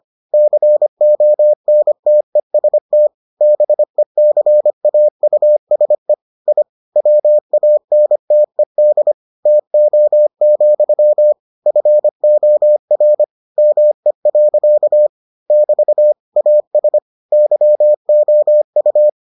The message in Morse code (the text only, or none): CONTEST BECAUSE I WANTED TO, FOR ME. <BT> AS YOU